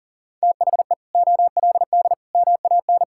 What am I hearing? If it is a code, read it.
Morse code, heard as a keyed tone, about 40 wpm, 700 Hz. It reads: THE OLD MAN
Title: Sentences from Top 1600 Words 40wpm